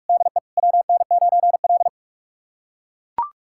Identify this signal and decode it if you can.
Morse code, heard as a keyed tone, about 42 wpm, 700 Hz. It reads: DE WN9L
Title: USA Callsign DE Callsign 45wpm